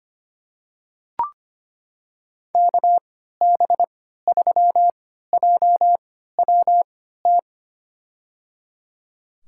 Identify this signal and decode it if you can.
Morse code, heard as a keyed tone, about 25 wpm, spaced as for 14 wpm, 700 Hz. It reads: KB3JWT
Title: USA Call Signs - 3x Character Spacing - 25wpm